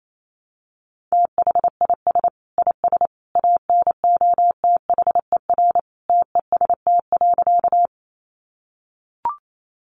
Morse code, heard as a keyed tone, about 28 wpm, 700 Hz. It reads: THIS IS ANOTHER TEST.
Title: Sentences from Top 600 Words 28wpm